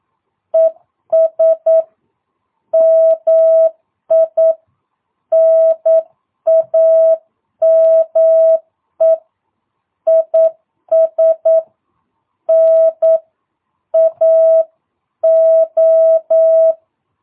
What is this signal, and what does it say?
Morse code, heard as a keyed tone, about 9 wpm, 650 Hz. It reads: ES MI NAME IS NAO
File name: jh8cbh / 2022cw